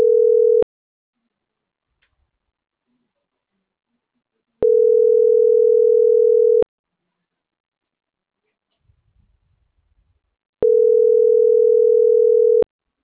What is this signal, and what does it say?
Ringback tone (North America), 440 + 480 Hz.
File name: CENTRAL MAERSK / out gabrielle.melendez - .4902